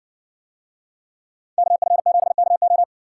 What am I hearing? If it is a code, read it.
Morse code, heard as a keyed tone, about 57 wpm, 700 Hz. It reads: KF7KO